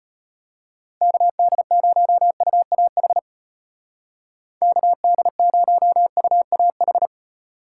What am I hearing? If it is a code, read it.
Morse code, heard as a keyed tone, about 38 wpm, 700 Hz. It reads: KD0UAH KD0UAH